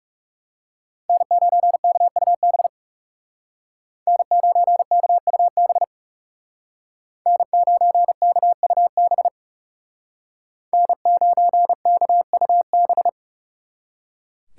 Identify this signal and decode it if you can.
Morse code, heard as a keyed tone, about 36 wpm, 700 Hz. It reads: N9KUB N9KUB N9KUB N9KUB
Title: Speed Racing - USA 1x3 Call Signs 30wpm